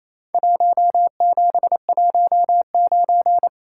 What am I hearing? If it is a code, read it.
Morse code, heard as a keyed tone, about 28 wpm, 700 Hz. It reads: 1719
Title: Worked All Europe DX Contest - QTC Exchange - 28wpm